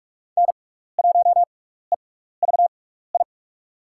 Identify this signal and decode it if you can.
Morse code, heard as a keyed tone, about 44 wpm, spaced as for 15 wpm, 700 Hz. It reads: N1EVI
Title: USA Call Signs - 5 Characters - 6x Character Spacing - 45wpm